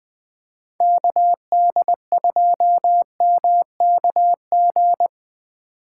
Morse code, heard as a keyed tone, about 20 wpm, 700 Hz. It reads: KD2MKG